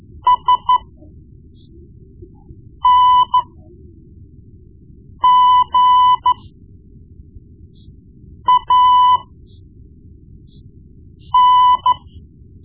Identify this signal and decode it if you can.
Morse code, heard as a keyed tone, about 13 wpm, spaced as for 4 wpm, 1000 Hz. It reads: SNGAN
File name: Morse Lessons / Morse Lessons / Tape-1B